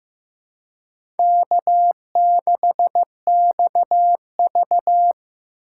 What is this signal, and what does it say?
Morse code, heard as a keyed tone, about 15 wpm, 700 Hz. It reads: K6XV
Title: USA Call Signs - 4 Characters - 4x Character Spacing - 15wpm